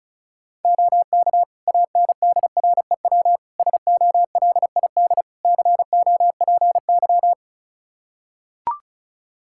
Morse code, heard as a keyed tone, about 36 wpm, 700 Hz. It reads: OK ANDREW SOLID COPY
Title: Practice QSO - Exchange 3 - More Info - 35wpm